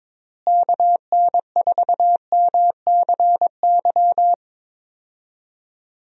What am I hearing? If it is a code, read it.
Morse code, heard as a keyed tone, about 22 wpm, 700 Hz. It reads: KN4MCY
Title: USA Call Signs - 6 Characters - Repeated 2x - 22wpm